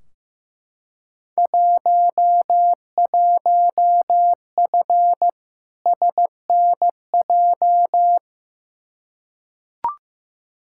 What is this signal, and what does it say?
Morse code, heard as a keyed tone, about 15 wpm, 700 Hz. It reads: 11F SNJ
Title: Field Day Exchanges Encore - 15wpm